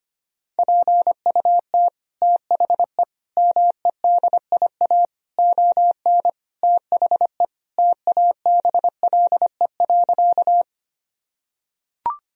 Morse code, heard as a keyed tone, about 25 wpm, 700 Hz. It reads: PUT THE MEDIA ON THE TABLE.
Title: Sentences from Top 1000 Words - Review Part 4 - 25wpm